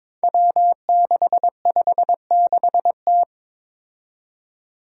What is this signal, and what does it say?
Morse code, heard as a keyed tone, about 22 wpm, 700 Hz. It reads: W656T
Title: Five Challenging Letter-Number Combinations 22wpm